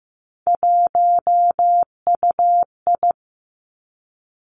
Morse code, heard as a keyed tone, about 15 wpm, 700 Hz. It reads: 1UI
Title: Three Challenging Letter-Number Combinations 15wpm